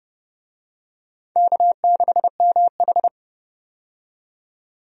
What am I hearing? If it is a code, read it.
Morse code, heard as a keyed tone, about 30 wpm, 700 Hz. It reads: K6MH
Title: USA Call Signs - 4 Characters - 30wpm